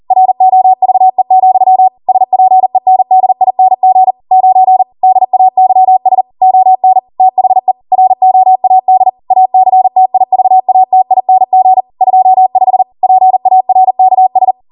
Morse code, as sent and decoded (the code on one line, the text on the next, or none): .-. --- ...- . --..-- / ... .--. -. -.. .. -. --. / ----. / -.. .- -.-- ... / --- -. / - .... / .-. --- .- -.. / .- -.-. - .. ...- .- - .. -. --. / ..--- ..... / .--. .- .-. -.- ...
ROVE, SPENDING 9 DAYS ON THE ROAD ACTIVATING 25 PARKS